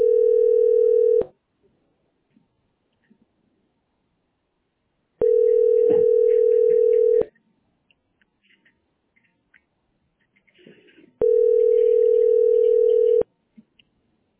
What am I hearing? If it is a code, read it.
Ringback tone (North America), 440 + 480 Hz.